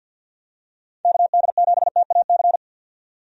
Morse code, heard as a keyed tone, about 53 wpm, 700 Hz. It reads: KD7TAC
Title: USA Call Signs - 6 Characters - 50wpm